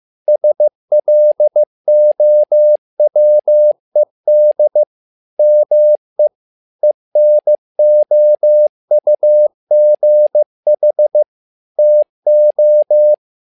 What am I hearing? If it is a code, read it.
Morse code, heard as a keyed tone, about 15 wpm, 600 Hz. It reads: SLOWED ME ENOUGH TO
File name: learning center / tower trouble / tower trouble 15wpm